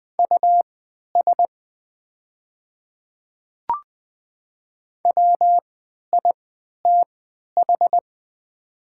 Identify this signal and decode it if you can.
Morse code, heard as a keyed tone, about 20 wpm, spaced as for 11 wpm, 700 Hz. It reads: US WITH
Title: 3x Character Spacing - Top 200 Words - 20wpm